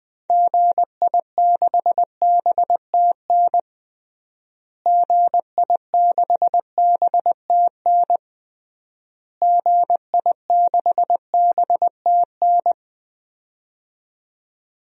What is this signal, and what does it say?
Morse code, heard as a keyed tone, about 20 wpm, 700 Hz. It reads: GI6BTN GI6BTN GI6BTN
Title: DXCC-100 Call Signs - 6 Characters - Repeated 3x - 20wpm